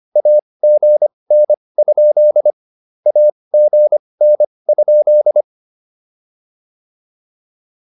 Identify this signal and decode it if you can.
Morse code, heard as a keyed tone, about 25 wpm, spaced as for 20 wpm, 600 Hz. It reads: AGN? AGN?